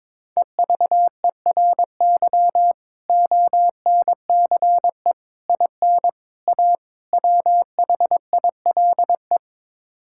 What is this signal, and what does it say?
Morse code, heard as a keyed tone, about 22 wpm, 700 Hz. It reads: EVERY ONCE IN A WHILE
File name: practice / 22wpm / Build A Phrase - 5-Word Phrases - 22wpm